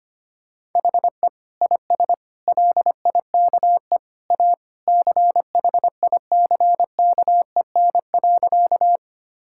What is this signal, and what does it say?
Morse code, heard as a keyed tone, about 25 wpm, 700 Hz. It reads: HE IS LIKE A CHICKEN.
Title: Sentences from Top 1000 Words - Review Part 5 - 25wpm